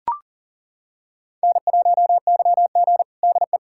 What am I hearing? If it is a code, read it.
Morse code, heard as a keyed tone, about 40 wpm, 700 Hz. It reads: N1YG DE
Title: USA Callsign DE Callsign - 4 Characters - 40wpm